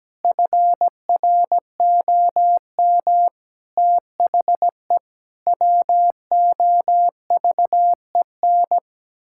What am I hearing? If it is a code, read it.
Morse code, heard as a keyed tone, about 17 wpm, 700 Hz. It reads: FROM THE WOVEN
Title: Pangram Challenge v2 - 17wpm